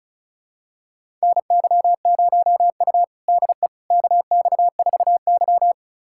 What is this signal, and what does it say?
Morse code, heard as a keyed tone, about 36 wpm, 700 Hz. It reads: NY0U DE KX4Y
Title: USA Callsign DE Callsign - 4 Characters - 35wpm